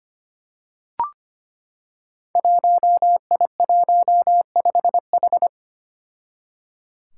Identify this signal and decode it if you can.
Morse code, heard as a keyed tone, about 25 wpm, 700 Hz. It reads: 1I15H
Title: Five Challenging Letter-Number Combinations 25wpm